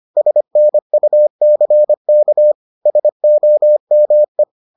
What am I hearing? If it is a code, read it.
Morse code, heard as a keyed tone, about 25 wpm, 600 Hz. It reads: SNUCK SOME